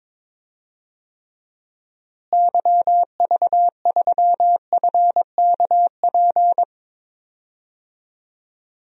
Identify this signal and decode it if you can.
Morse code, heard as a keyed tone, about 23 wpm, 700 Hz. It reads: YV3FKP
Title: DXCC-100 Call Signs - 6 Characters - Repeated 2x - 22wpm